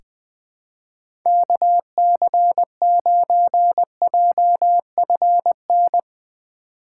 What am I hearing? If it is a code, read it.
Morse code, heard as a keyed tone, about 20 wpm, 700 Hz. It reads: KC9JFN
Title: USA Call Signs - 6 Characters - 3x Character Spacing - 20wpm